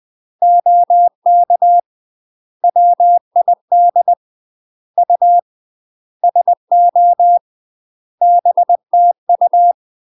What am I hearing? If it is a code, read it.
Morse code, heard as a keyed tone, about 20 wpm, 700 Hz. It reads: OK WID U SO BTU
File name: UK MASTER QSOs TRAINING/20WPM